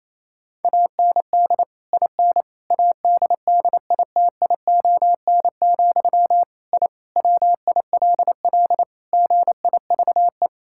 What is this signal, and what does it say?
Morse code, heard as a keyed tone, about 27 wpm, 700 Hz. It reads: AND IN ADDITION, I WILL GIVE
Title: Sentences from Top 1000 Words - Review Part 6 - 28wpm